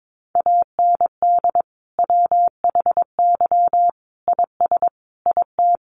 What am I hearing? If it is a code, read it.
Morse code, heard as a keyed tone, about 22 wpm, 700 Hz. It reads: AND WHY IS IT
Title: Sentences from Top 1100 Words 22wpm